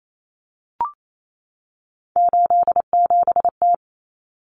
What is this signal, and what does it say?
Morse code, heard as a keyed tone, about 28 wpm, 700 Hz. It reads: 87T
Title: Common Cut Numbers - 3-Digit 28wpm